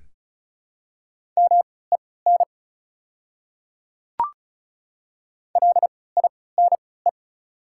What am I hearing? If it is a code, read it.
Morse code, heard as a keyed tone, about 36 wpm, spaced as for 20 wpm, 700 Hz. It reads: MEN LINE
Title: 3x Character Spacing - Top 200 Words - 35wpm